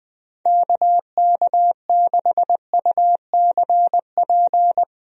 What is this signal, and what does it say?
Morse code, heard as a keyed tone, about 20 wpm, 700 Hz. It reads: KK6UCP
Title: USA Call Signs - 6 Characters - Repeated 2x - 20wpm